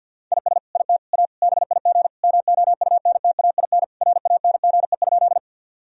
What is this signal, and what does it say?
Morse code, heard as a keyed tone, about 50 wpm, 700 Hz. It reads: IS IT A BIG MOUNTAIN RANGE?
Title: Sentences from Top 1000 Words - Review Part 4 - 50wpm